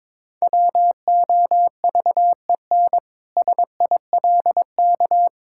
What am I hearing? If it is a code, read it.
Morse code, heard as a keyed tone, about 22 wpm, 700 Hz. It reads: WOVEN SILK
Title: Pangram Challenge v2 - 22wpm